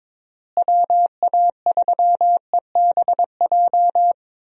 Morse code, heard as a keyed tone, about 21 wpm, 700 Hz. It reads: WA3EBJ
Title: USA Call Signs - 6 Characters - Repeated 3x - 22wpm